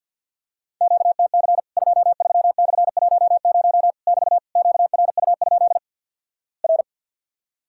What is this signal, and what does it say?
Morse code, heard as a keyed tone, about 50 wpm, 700 Hz. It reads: QTC 23/10 <BT> QRU? R
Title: Worked All Europe DX Contest - QTC Exchange with Repeats - 50wpm